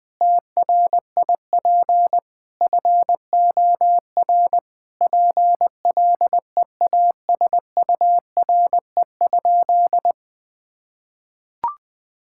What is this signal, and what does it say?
Morse code, heard as a keyed tone, about 19 wpm, 700 Hz. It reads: TRIP FOR PLEASURE?
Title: Sentences from Top 1600 Words 20wpm